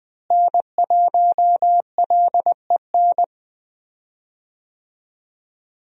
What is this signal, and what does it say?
Morse code, heard as a keyed tone, about 20 wpm, 700 Hz. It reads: N1LEN
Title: USA Call Signs - 5 Characters - Repeated 3x - 20wpm